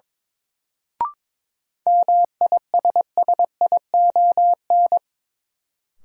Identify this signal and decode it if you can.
Morse code, heard as a keyed tone, about 22 wpm, 700 Hz. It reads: MISSION
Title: Sentences from Top 1600 Words - New Word Drill - 22wpm